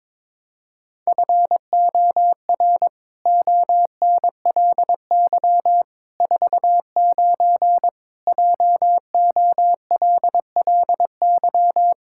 Morse code, heard as a keyed tone, about 22 wpm, 700 Hz. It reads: FOR ONLY 49 JOLLY
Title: Pangram Challenge v2 - 22wpm